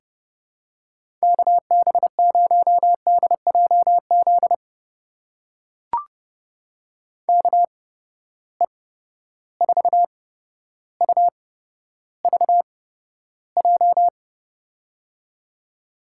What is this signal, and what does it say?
Morse code, heard as a keyed tone, about 30 wpm, spaced as for 13 wpm, 700 Hz. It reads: KB0DJZ KE4UVJ
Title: USA Call Signs - 6 Characters - 8x Character Spacing - 30wpm